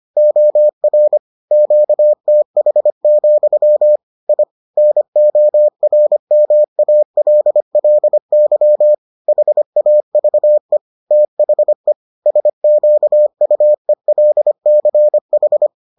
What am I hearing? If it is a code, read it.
Morse code, heard as a keyed tone, about 25 wpm, 600 Hz. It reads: OR QTH, I NORMALLY HAVE THE SQUELCH